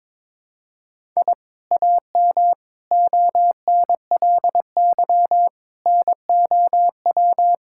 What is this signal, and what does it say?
Morse code, heard as a keyed tone, about 21 wpm, 700 Hz. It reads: I AM ONLY NOW
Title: Sentences from Top 1000 Words - Review Part 5 - 22wpm